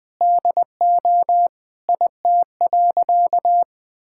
Morse code, heard as a keyed tone, about 20 wpm, 700 Hz. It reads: DO IT.
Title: Sentences with 3-Word Phrases - 20wpm